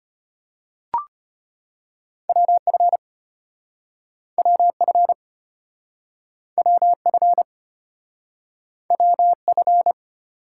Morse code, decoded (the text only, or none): WF WF WF WF